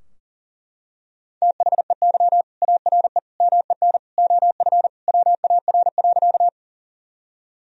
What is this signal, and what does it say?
Morse code, heard as a keyed tone, about 40 wpm, 700 Hz. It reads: THEY ARE MEN OF WAR.
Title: Sentences from Top 1000 Words - Review Part 2 - 40wpm